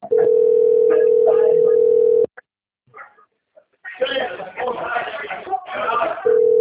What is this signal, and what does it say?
Ringback tone (North America), 440 + 480 Hz, with speech.
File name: out- - - - - .2770